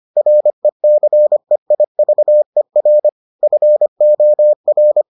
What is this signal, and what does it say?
Morse code, heard as a keyed tone, about 25 wpm, 600 Hz. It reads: RECEIVER FOR